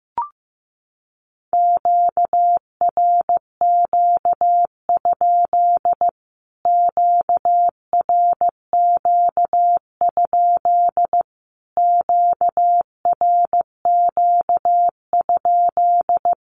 Morse code, heard as a keyed tone, about 15 wpm, 700 Hz. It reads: QRQ? QRQ? QRQ?
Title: Q-Codes Repeated 3x 15wpm